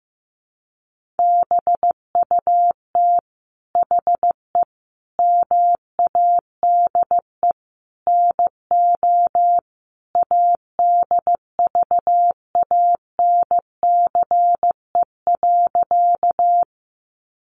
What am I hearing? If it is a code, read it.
Morse code, heard as a keyed tone, about 15 wpm, 700 Hz. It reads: BUT HE MADE NO ADVANCE.
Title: Sentences from Top 1400 Words 15wpm